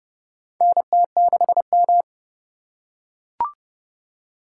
Morse code, heard as a keyed tone, about 30 wpm, 700 Hz. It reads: NT6M
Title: USA Call Signs - 3x Character Spacing - 30wpm